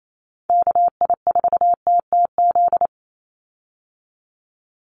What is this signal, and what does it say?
Morse code, heard as a keyed tone, about 28 wpm, 700 Hz. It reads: KI4TTZ
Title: USA Call Signs 28wpm